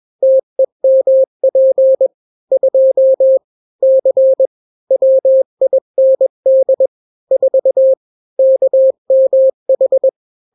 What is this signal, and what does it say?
Morse code, heard as a keyed tone, about 21 wpm, 530 Hz. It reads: TEMP 2 C WIND 4 KMH
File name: daily / 20wpm